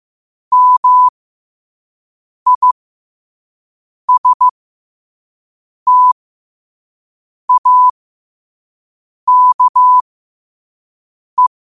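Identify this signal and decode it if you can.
Morse code, heard as a keyed tone, about 15 wpm, spaced as for 5 wpm, 1000 Hz. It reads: MISTAKE
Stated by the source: Morse at 5 wpm.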